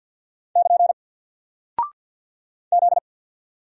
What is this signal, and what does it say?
Morse code, heard as a keyed tone, about 48 wpm, 700 Hz. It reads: <KN> Z
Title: Single Character (Letter:Number:Punctuation:Prosign) 50wpm